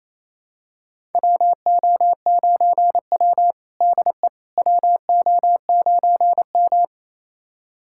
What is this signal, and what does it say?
Morse code, heard as a keyed tone, about 28 wpm, 700 Hz. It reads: WO9W DE WO9M
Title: USA Callsign DE Callsign - 4 Characters - 28wpm